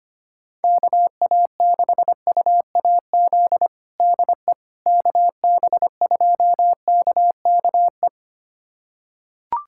Morse code, heard as a keyed tone, about 25 wpm, 700 Hz. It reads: KA6UAZ DE KB2KKE
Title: USA Callsign DE Callsign - 6 Characters - 25wpm